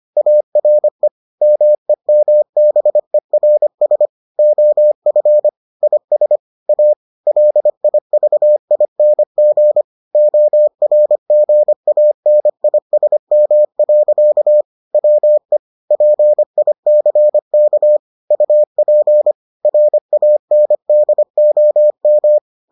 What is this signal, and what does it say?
Morse code, heard as a keyed tone, about 25 wpm, 600 Hz. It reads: ARE MEMBERS OF IS A LIVING ORGANISM. WE PICK UP RANDOM